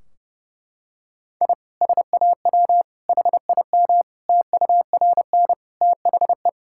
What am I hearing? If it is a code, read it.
Morse code, heard as a keyed tone, about 30 wpm, 700 Hz. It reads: I SAW HIM TURN THE
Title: Sentences from Top 1600 Words 30wpm